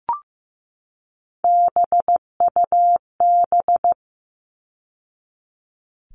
Morse code, heard as a keyed tone, about 15 wpm, 700 Hz. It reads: BUB
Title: All Cut Numbers - 3-Digit 15wpm